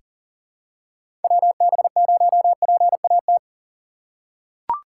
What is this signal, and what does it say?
Morse code, heard as a keyed tone, about 40 wpm, 700 Hz. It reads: WB0PAT
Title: USA Call Signs - 6 Characters - 2x Character Spacing - 40wpm